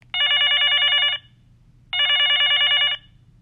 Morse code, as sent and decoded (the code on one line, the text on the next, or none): none